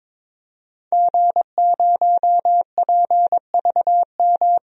Morse code, heard as a keyed tone, about 22 wpm, 700 Hz. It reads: G0PVM
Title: DXCC-100 Call Signs - 5 Characters - 22wpm